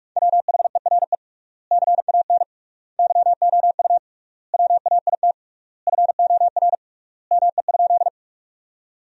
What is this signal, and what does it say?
Morse code, heard as a keyed tone, about 43 wpm, 700 Hz. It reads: WHERE CAN YOU WAIT FOR ME?
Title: 3x Word-Spacing - Sentences from Top 500 Words - 45wpm